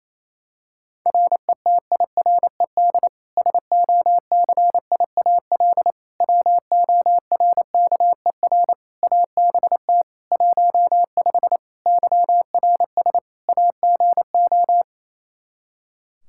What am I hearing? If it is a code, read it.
Morse code, heard as a keyed tone, about 28 wpm, 700 Hz. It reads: RETIRED SOCIAL WORKER ABT 15 YRS AGO
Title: Practice QSO - Exchange 3 - More Info - 28wpm